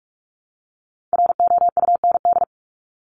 Morse code, heard as a keyed tone, about 44 wpm, 700 Hz. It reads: ROUND
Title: Top 200 Words Repeated 45wpm